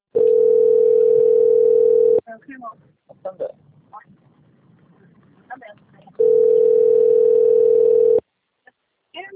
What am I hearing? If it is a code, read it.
Ringback tone (North America), 440 + 480 Hz, with speech.
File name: out- - - - - .20188